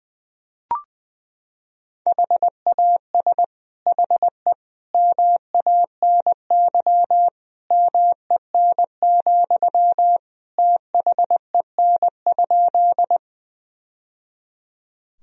Morse code, heard as a keyed tone, about 20 wpm, 700 Hz. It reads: HAS HE MANY MEN, THEN?
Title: Sentences from Top 1000 Words - Review Part 1 - 20wpm